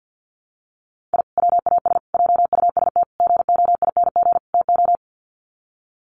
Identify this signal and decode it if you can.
Morse code, heard as a keyed tone, about 50 wpm, 700 Hz. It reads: I WAS JUST GOING TO
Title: Sets of 5 Words Encore 50wpm